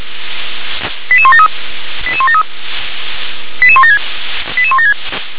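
Five-tone selective call (ZVEI), heard as quick runs of five tones, tones of about 70 ms. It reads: 88173 88173 88176 88176
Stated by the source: ZVEI five-tone selective call received by SDR.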